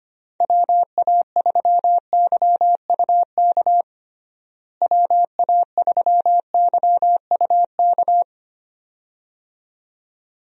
Morse code, heard as a keyed tone, about 25 wpm, 700 Hz. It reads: WA3YUK WA3YUK
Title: USA Call Signs - 6 Characters - Repeated 2x - 25wpm